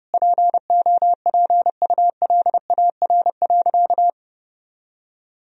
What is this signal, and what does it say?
Morse code, heard as a keyed tone, about 30 wpm, 700 Hz. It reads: POPULAR.
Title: Sentences from Top 1000 Words - Review Part 3 - 30wpm